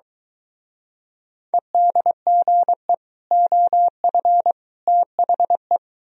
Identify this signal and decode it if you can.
Morse code, heard as a keyed tone, about 23 wpm, 700 Hz. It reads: EDGE OF THE